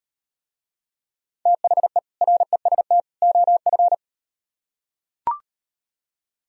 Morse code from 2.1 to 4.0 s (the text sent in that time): REST OF